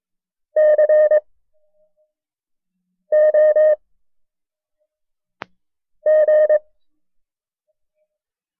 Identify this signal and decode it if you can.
Morse code, heard as a keyed tone, about 22 wpm, spaced as for 4 wpm, 610 Hz. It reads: COG
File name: FreeFiles / CW-practice / KB6MT-tape1-side2